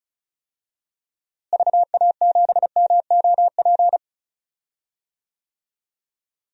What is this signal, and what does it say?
Morse code, heard as a keyed tone, about 36 wpm, 700 Hz. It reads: VA7MOP
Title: DXCC-100 Call Signs - 6 Characters - Repeated 3x - 35wpm